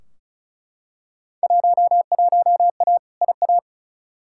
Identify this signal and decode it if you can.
Morse code, heard as a keyed tone, about 36 wpm, 700 Hz. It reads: 11A IA
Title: Field Day Exchanges Encore - 35wpm